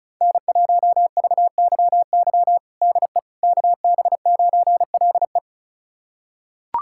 Morse code, heard as a keyed tone, about 34 wpm, 700 Hz. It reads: N1VYY DE KB9LE
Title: USA Callsign DE Callsign - 5 Characters - 35wpm